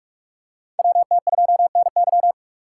Morse code, heard as a keyed tone, about 47 wpm, 700 Hz. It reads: WT2NY
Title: USA Call Signs - 4x Character Spacing - 45wpm